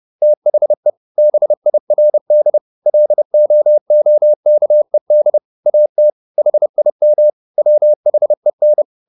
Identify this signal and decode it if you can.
Morse code, heard as a keyed tone, about 30 wpm, 600 Hz. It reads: THE BIRD LOOKED AT HIM WHEN
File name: the old man and the sea / the old man and the sea 1 30wpm